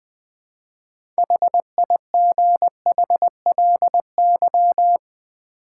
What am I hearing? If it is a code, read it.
Morse code, heard as a keyed tone, about 20 wpm, 700 Hz. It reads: HIGHLY
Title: Sentences from Top 1100 Words - New Word Drill Alt v2 - 20wpm